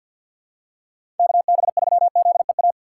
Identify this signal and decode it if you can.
Morse code, heard as a keyed tone, about 50 wpm, 700 Hz. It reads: KB3ZEA
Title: USA Call Signs - 6 Characters - 3x Character Spacing - 50wpm